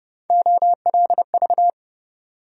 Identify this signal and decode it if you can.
Morse code, heard as a keyed tone, about 29 wpm, 700 Hz. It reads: OLV
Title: USA Call Sign Postfixes - 3 Letters 30wpm